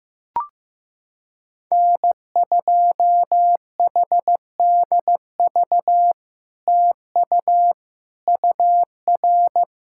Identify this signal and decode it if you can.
Morse code, heard as a keyed tone, about 15 wpm, 700 Hz. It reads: N2HDV TU UR
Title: SOTA QSO Practice 15wpm